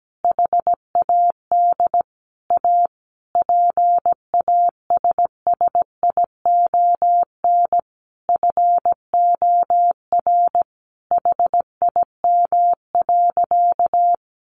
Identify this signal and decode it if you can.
Morse code, heard as a keyed tone, about 17 wpm, 700 Hz. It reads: HAD A PASSION FOR HIM.
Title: Sentences from Top 1300 Words 17wpm